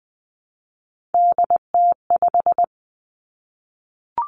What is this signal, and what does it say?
Morse code, heard as a keyed tone, about 20 wpm, 700 Hz. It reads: DT5
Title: Three Challenging Letter-Number Combinations 20wpm